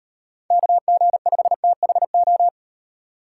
Morse code, heard as a keyed tone, about 38 wpm, 700 Hz. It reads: KG5THO